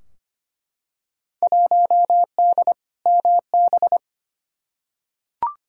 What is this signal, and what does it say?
Morse code, heard as a keyed tone, about 25 wpm, 700 Hz. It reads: 1D MB
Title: Field Day Exchanges Encore - 25wpm